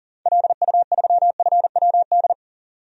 Morse code, heard as a keyed tone, about 40 wpm, 700 Hz. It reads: LU3FWD
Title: DXCC-100 Call Signs - 6 Characters - 40wpm